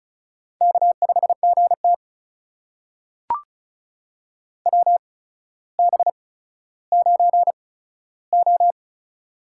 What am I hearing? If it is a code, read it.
Morse code, heard as a keyed tone, about 36 wpm, spaced as for 15 wpm, 700 Hz. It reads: K5GT WB9O
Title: USA Call Signs - 4 Characters - 8x Character Spacing - 35wpm